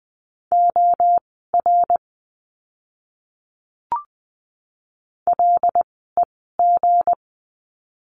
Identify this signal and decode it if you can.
Morse code, heard as a keyed tone, about 20 wpm, spaced as for 14 wpm, 700 Hz. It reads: OR LEG
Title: 160 Lesson 26f - Letter g - Words 20wpm